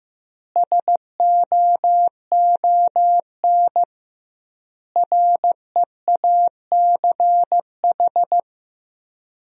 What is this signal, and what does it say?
Morse code, heard as a keyed tone, about 15 wpm, 700 Hz. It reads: SOON REACH
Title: 2x Word-Spacing - Sentences from Top 400 Words - 15wpm